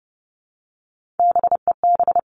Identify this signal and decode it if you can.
Morse code, heard as a keyed tone, about 30 wpm, 700 Hz. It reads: BEB
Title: All Cut Numbers - 3-Digit 30wpm